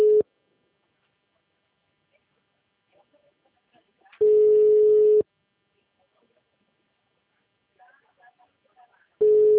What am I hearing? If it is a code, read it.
Ringback tone (Europe, 425 Hz).
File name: out- -2018- - - .10427